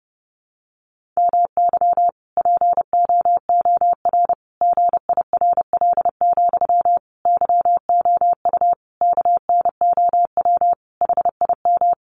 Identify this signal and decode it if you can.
Morse code, heard as a keyed tone, about 30 wpm, 700 Hz. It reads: MY POOR GIRL, YOU KNOW HIM
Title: Sentences from Top 1000 Words 30wpm